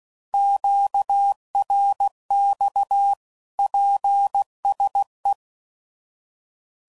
Morse code, heard as a keyed tone, about 16 wpm, 800 Hz. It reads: QRX PSE
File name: CW Code Groups @15WPM / QSOs / kc7haa